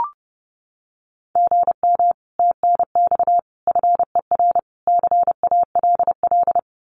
Morse code, heard as a keyed tone, about 30 wpm, 700 Hz. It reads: GM TNX FER CALL
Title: Common QSO Phrases Encore 30wpm